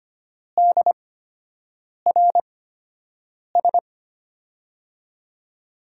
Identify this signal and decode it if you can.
Morse code, heard as a keyed tone, about 25 wpm, spaced as for 7 wpm, 700 Hz. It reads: DRS
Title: USA Call Signs - 5 Characters - 8x Character Spacing - 25wpm